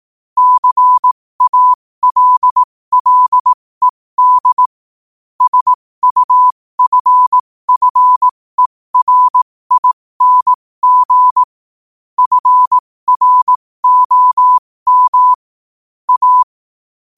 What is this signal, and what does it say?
Morse code, heard as a keyed tone, about 18 wpm, spaced as for 15 wpm, 1000 Hz. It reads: CALLED SUFFERING FROM A
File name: qotdfiles / 15wpm